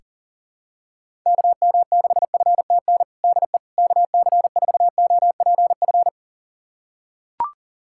Morse code, heard as a keyed tone, about 40 wpm, 700 Hz. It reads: KM6FTN DE KC4OPF
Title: USA Callsign DE Callsign 40wpm